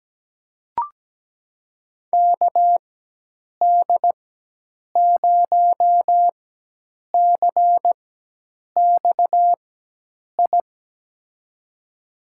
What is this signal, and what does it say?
Morse code, heard as a keyed tone, about 17 wpm, spaced as for 8 wpm, 700 Hz. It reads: KD0CXI
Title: USA Call Signs - 4x Character Spacing - 17wpm